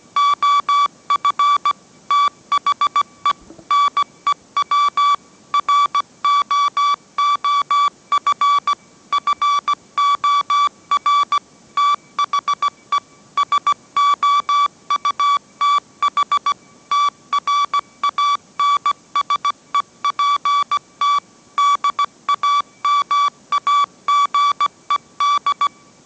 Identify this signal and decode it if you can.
Morse code, heard as a keyed tone, about 17 wpm, 1150 Hz. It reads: OF THE NEW ROOF FOR THE SOUTH TRANSEPT DAMAGED
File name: mrx / mrx080 18wpm